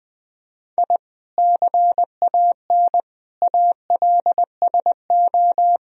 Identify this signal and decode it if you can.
Morse code, heard as a keyed tone, about 20 wpm, 700 Hz. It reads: I CAN ALSO
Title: Sentences from Top 1000 Words - Review Part 2 - 20wpm